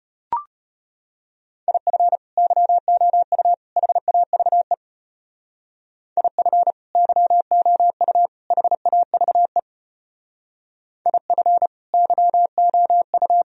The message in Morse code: IF YOU HAVE IF YOU HAVE IF YOU

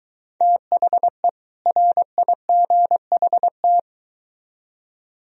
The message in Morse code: THE RIGHT

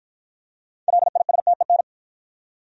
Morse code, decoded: LISTEN

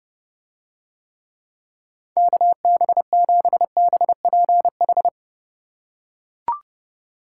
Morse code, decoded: KB7BPH

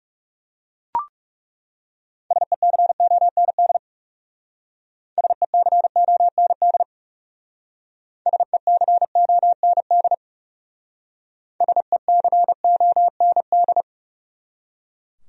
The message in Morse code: SECOND SECOND SECOND SECOND

